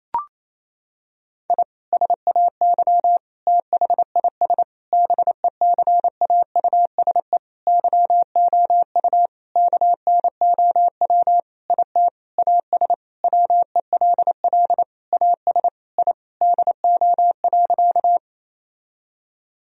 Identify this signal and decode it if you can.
Morse code, heard as a keyed tone, about 27 wpm, 700 Hz. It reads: I SAY THIS BECAUSE YOU KNOW IT AS WELL AS I DO.
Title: Sentences from Top 1000 Words - Review Part 3 - 28wpm